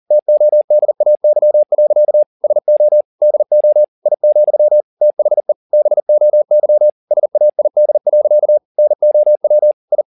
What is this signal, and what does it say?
Morse code, heard as a keyed tone, about 40 wpm, 600 Hz. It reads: TODAY. SO DO I, THE BOY SAID. NOW I